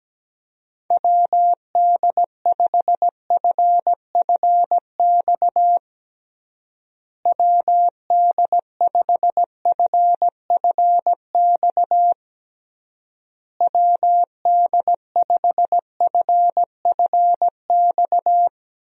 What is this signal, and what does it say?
Morse code, heard as a keyed tone, about 17 wpm, 700 Hz. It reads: WD5FFX WD5FFX WD5FFX
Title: USA Call Signs - 6 Characters - Repeated 3x - 17wpm